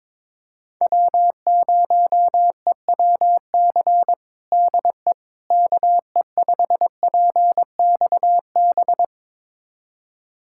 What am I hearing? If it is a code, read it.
Morse code, heard as a keyed tone, about 22 wpm, 700 Hz. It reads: W0EWC DE KE5PXB
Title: USA Callsign DE Callsign 22wpm